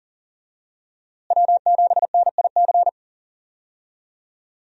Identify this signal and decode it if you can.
Morse code, heard as a keyed tone, about 40 wpm, 700 Hz. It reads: W7NIC
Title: USA Call Signs - 5 Characters - Repeated 3x - 40wpm